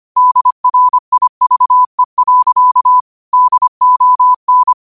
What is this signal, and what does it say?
Morse code, heard as a keyed tone, about 25 wpm, 1000 Hz. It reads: DRIVE. DON
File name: qotdfiles / 25wpm